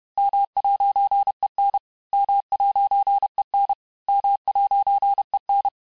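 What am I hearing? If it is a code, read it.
Morse code, heard as a keyed tone, about 31 wpm, 800 Hz. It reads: M'EN M'EN M'EN
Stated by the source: Morse lesson of words, each word repeated 4 times, at 29 words per minute.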